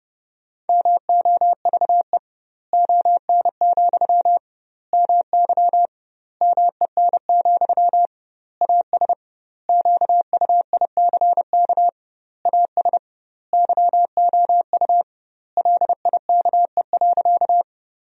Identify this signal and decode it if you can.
Morse code, heard as a keyed tone, about 30 wpm, 700 Hz. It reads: MOVE ON, MY MEN, AS QUICK AS YOU LIKE.
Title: 2x Word-Spacing - Sentences from Top 500 Words - 30wpm